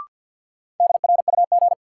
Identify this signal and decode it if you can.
Morse code, heard as a keyed tone, about 52 wpm, 700 Hz. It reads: DRUG
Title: Sets of 5 Words - New Word Drill - 50wpm